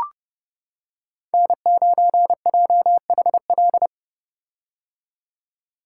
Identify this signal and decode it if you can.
Morse code, heard as a keyed tone, about 30 wpm, 700 Hz. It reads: N9JHL
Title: K1USN SST QSO Practice - 30wpm